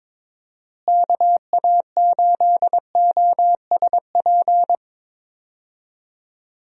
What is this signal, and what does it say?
Morse code, heard as a keyed tone, about 21 wpm, 700 Hz. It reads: KA8OSP
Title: USA Call Signs - 6 Characters - Repeated 2x - 22wpm